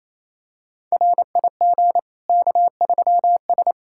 Morse code, heard as a keyed tone, about 28 wpm, 700 Hz. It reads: RIG K3S
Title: Practice QSO - Exchange 2 - Info - 28wpm